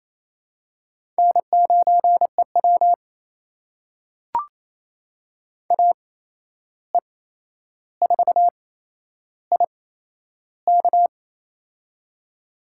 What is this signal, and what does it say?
Morse code, heard as a keyed tone, about 28 wpm, spaced as for 11 wpm, 700 Hz. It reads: N9EW AE4IK
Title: USA Call Signs - 8x Character Spacing - 28wpm